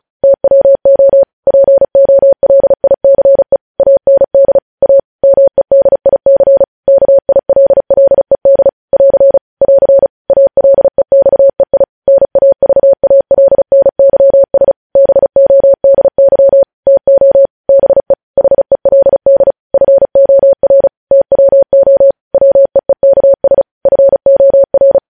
Morse code, heard as a keyed tone, about 36 wpm, 550 Hz. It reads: TWO POLICE AND A MEDIC KILLED <AR> <AR> ALEXEI NAVALNYS BODY TO BE HELD FOR TWO WEEKS FOR